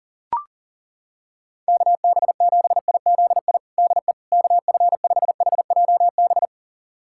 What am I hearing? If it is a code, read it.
Morse code, heard as a keyed tone, about 40 wpm, 700 Hz. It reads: KB7IZI DE KF5HJB
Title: USA Callsign DE Callsign - 6 Characters - 40wpm